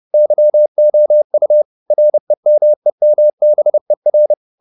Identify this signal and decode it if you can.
Morse code, heard as a keyed tone, about 30 wpm, 600 Hz. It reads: YOU REMEMBER
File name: the old man and the sea / the old man and the sea 1 30wpm